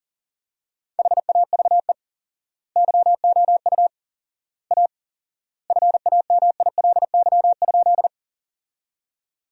Morse code, heard as a keyed tone, about 40 wpm, 700 Hz. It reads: HAVE YOU A FAMILY?
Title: 4x Word-Spacing - Sentences from Top 400 Words - 40wpm